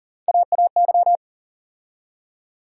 Morse code, heard as a keyed tone, about 40 wpm, 700 Hz. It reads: AAY
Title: USA Call Sign Postfixes - 3 Letters 40wpm